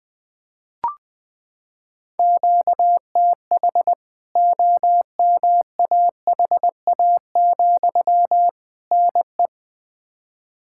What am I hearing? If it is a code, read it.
Morse code, heard as a keyed tone, about 21 wpm, 700 Hz. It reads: QTH OMAHA, NE
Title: K1USN SST QSO Practice 20wpm